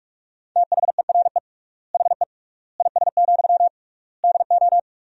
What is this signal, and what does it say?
Morse code, heard as a keyed tone, about 46 wpm, 700 Hz. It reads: THERE HE IS, DO
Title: 3x Word-Spacing - Sentences from Top 100 Words - 45wpm